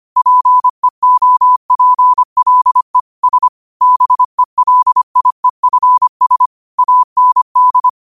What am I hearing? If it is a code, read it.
Morse code, heard as a keyed tone, about 25 wpm, 1000 Hz. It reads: PEOPLE S BELIEFS AND